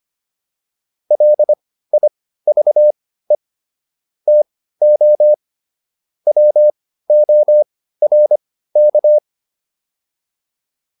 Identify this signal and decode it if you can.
Morse code, heard as a keyed tone, about 25 wpm, spaced as for 15 wpm, 600 Hz. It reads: LIVE TO WORK